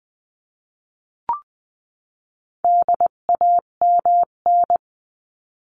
Morse code, heard as a keyed tone, about 20 wpm, 700 Hz. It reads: DAMN
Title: 258 Lesson 42c - Words 16-20wpm